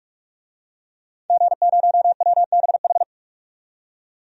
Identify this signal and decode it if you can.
Morse code, heard as a keyed tone, about 48 wpm, 700 Hz. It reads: G0WBH
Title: DXCC-100 Call Signs - 5 Characters - 45wpm